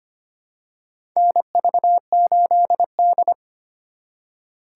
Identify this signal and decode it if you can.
Morse code, heard as a keyed tone, about 25 wpm, 700 Hz. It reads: NV8D
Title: USA Call Signs - 4 Characters - 4x Character Spacing - 25wpm